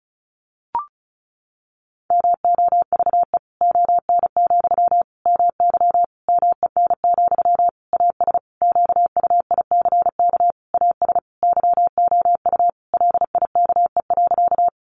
Morse code, heard as a keyed tone, about 34 wpm, 700 Hz. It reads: MOVE ON, MY MEN, AS QUICK AS YOU LIKE.
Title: Sentences from Top 500 Words 35wpm